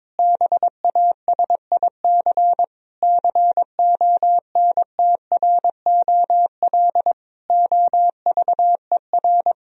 Morse code, heard as a keyed tone, about 22 wpm, 700 Hz. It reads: BASIC CONTROL OVER
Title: Sentences from Top 1000 Words - Review Part 3 - 22wpm